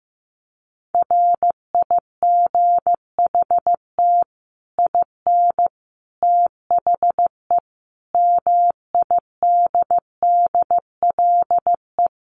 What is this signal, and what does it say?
Morse code, heard as a keyed tone, about 15 wpm, 700 Hz. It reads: RIGHT IN THE MIDDLE
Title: Sets of 5 Words 15wpm